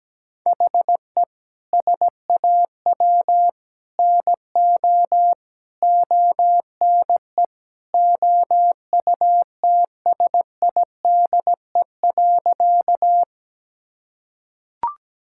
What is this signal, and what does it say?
Morse code, heard as a keyed tone, about 17 wpm, 700 Hz. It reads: HE SAW NO ONE OUTSIDE.
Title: Sentences from Top 1000 Words - Review Part 6 - 17wpm